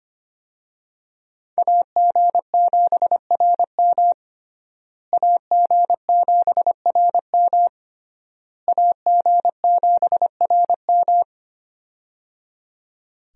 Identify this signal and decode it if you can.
Morse code, heard as a keyed tone, about 25 wpm, 700 Hz. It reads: AG7RM AG7RM AG7RM
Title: USA Call Signs - 5 Characters - Repeated 3x - 25wpm